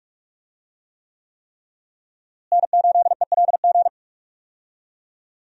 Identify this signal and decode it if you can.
Morse code, heard as a keyed tone, about 45 wpm, 700 Hz. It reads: N8ELG
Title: USA Call Signs - 5 Characters - Repeated 3x - 45wpm